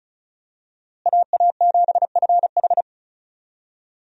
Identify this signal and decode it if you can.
Morse code, heard as a keyed tone, about 34 wpm, 700 Hz. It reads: AA7FH
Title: USA Call Signs - 5 Characters - Repeated 3x - 35wpm